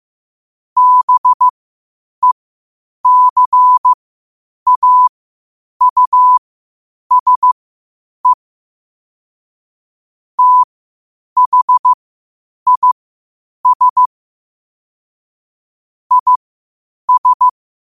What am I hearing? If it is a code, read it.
Morse code, heard as a keyed tone, about 15 wpm, spaced as for 8 wpm, 1000 Hz. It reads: BECAUSE THIS IS
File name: qotdfiles / 8wpm